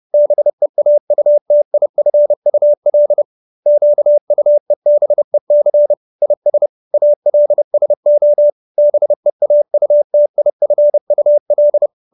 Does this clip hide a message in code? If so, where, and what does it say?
Yes: Morse code from 0.1 to 11.9 s, heard as a keyed tone, about 30 wpm, 590 Hz. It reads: BEAUTIFUL QUEBEC IS ALSO BEAUTIFUL